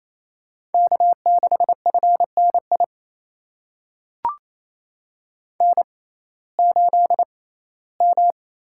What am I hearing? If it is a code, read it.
Morse code, heard as a keyed tone, about 28 wpm, spaced as for 17 wpm, 700 Hz. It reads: K6FNI N8M
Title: USA Call Signs - 5 Characters - 6x Character Spacing - 28wpm